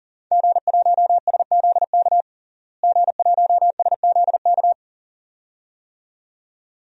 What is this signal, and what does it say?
Morse code, heard as a keyed tone, about 40 wpm, 700 Hz. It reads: G1SZK G1SZK
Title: DXCC-100 Call Signs - 5 Characters - Repeated 2x - 40wpm